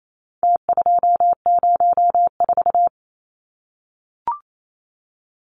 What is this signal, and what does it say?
Morse code, heard as a keyed tone, about 28 wpm, 700 Hz. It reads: T204
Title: Numbers - Contest Style - 4-Digit 28wpm